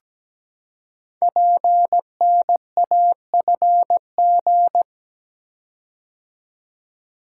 Morse code, heard as a keyed tone, about 17 wpm, 700 Hz. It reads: PNAFG
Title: Code Groups - 5 Letters - 17wpm